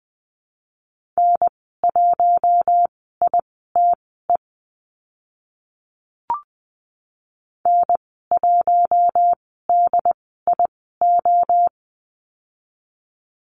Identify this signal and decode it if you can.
Morse code, heard as a keyed tone, about 20 wpm, spaced as for 14 wpm, 700 Hz. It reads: N1ITE N1DIO
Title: 082 Lesson 14g - Number 2 - Call Signs 20wpm